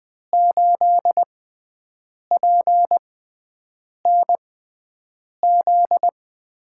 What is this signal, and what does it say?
Morse code, heard as a keyed tone, about 20 wpm, spaced as for 7 wpm, 700 Hz. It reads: 8PNZ
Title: USA Call Signs - 5 Characters - 6x Character Spacing - 20wpm